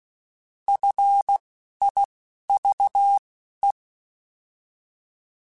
Morse code, heard as a keyed tone, about 16 wpm, spaced as for 11 wpm, 800 Hz. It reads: FIVE
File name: F Runs / F Runs mp3s / F Runs @10